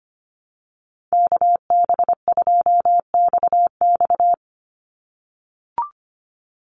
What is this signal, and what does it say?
Morse code, heard as a keyed tone, about 25 wpm, 700 Hz. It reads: KB2XX
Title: NA SOTA Chaser Callsigns 25wpm